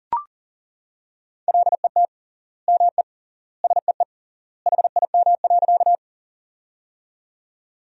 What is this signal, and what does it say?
Morse code, heard as a keyed tone, about 40 wpm, 700 Hz. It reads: LET ME SEE HIM.
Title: 3x Word-Spacing - Sentences from Top 300 Words - 40wpm